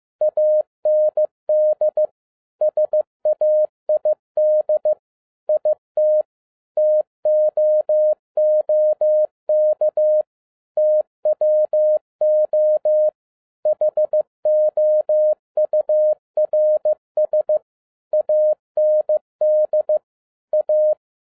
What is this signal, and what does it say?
Morse code, heard as a keyed tone, about 15 wpm, 610 Hz. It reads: AND SAID IT TOOK TWO HOURS AND A
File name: 2011-07-mp3 / n47 JOHN K3TN 15wpm